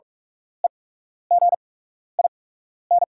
Morse code, heard as a keyed tone, about 43 wpm, spaced as for 12 wpm, 690 Hz. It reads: EGIN
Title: 8x Character Spacing - Top 500 Words - 45wpm